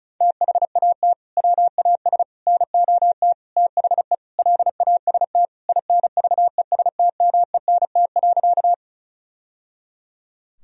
Morse code, heard as a keyed tone, about 36 wpm, 700 Hz. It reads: THAT WAS NOT THE LAST INVESTMENT.
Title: Sentences from Top 1000 Words - Review Part 4 - 35wpm